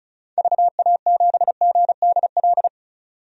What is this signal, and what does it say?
Morse code, heard as a keyed tone, about 36 wpm, 700 Hz. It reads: VA7GDL
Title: DXCC-100 Call Signs - 6 Characters - 35wpm